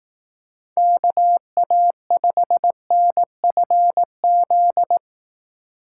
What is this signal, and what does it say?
Morse code, heard as a keyed tone, about 18 wpm, 700 Hz. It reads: KA5NFZ
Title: Speed Racing - USA 2x3 Call Signs 15wpm